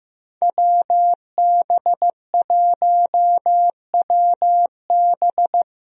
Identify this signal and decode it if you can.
Morse code, heard as a keyed tone, about 16 wpm, 700 Hz. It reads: WB1WB
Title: USA Call Signs - 5 Characters - Repeated 2x - 15wpm